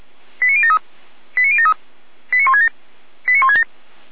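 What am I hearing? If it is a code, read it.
Five-tone selective call (ZVEI), heard as quick runs of five tones, tones of about 70 ms. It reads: 89073 89073 89167 89167
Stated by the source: ZVEI five-tone selective call received by SDR, from the feuerwehr-orlamuende.org fire brigade server.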